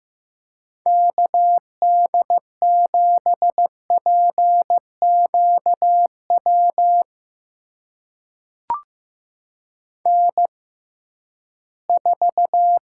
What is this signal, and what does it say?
Morse code, heard as a keyed tone, about 15 wpm, spaced as for 11 wpm, 700 Hz. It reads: KD7PQW N4
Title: USA Call Signs - 6x Character Spacing - 15wpm